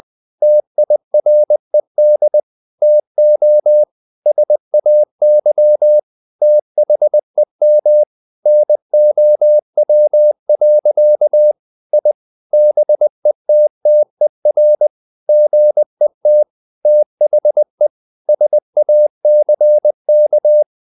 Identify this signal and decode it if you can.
Morse code, heard as a keyed tone, about 20 wpm, 600 Hz. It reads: TIRED TO SAY THEM NOW. I BETTER GET THE SACK
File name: the old man and the sea part 2 20wpm